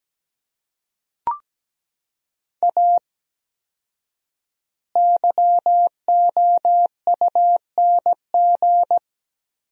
Morse code, heard as a keyed tone, about 17 wpm, 700 Hz. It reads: A YOUNG
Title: 4x Word-Spacing - Sentences from Top 400 Words - 17wpm